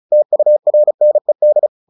Morse code, heard as a keyed tone, about 34 wpm, 600 Hz. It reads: TURNED
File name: the old man and the sea 2 35wpm